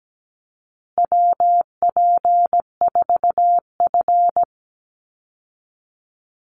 Morse code, heard as a keyed tone, about 17 wpm, 700 Hz. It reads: WP4F
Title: Call Signs Repeated 2x - 17wpm